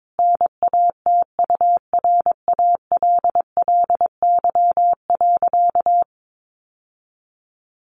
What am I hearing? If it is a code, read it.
Morse code, heard as a keyed tone, about 22 wpm, 700 Hz. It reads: NATURALLY.
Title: Sentences from Top 1300 Words 22wpm